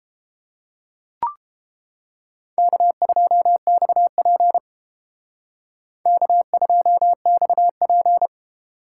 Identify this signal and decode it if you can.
Morse code, heard as a keyed tone, about 30 wpm, 700 Hz. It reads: K2XP K2XP